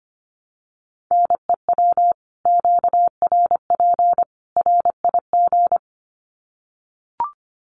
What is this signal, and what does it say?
Morse code, heard as a keyed tone, about 25 wpm, 700 Hz. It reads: NEW QRP RIG
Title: Common QSO Phrases 25wpm